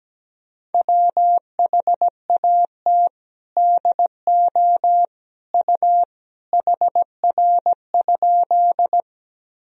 Morse code, heard as a keyed tone, about 17 wpm, 700 Hz. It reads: WHAT DO U HR?
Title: Sentences from Top 300 Words and Abbreviations 17wpm